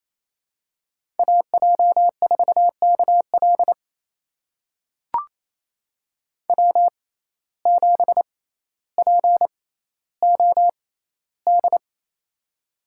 Morse code, heard as a keyed tone, about 28 wpm, spaced as for 14 wpm, 700 Hz. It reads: AJ4KL W7POD
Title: USA Call Signs - 5 Characters - 6x Character Spacing - 28wpm